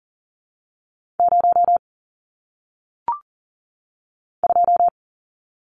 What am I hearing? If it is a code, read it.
Morse code, heard as a keyed tone, about 39 wpm, 700 Hz. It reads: 0 2
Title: Numbers - 1-Digit 40wpm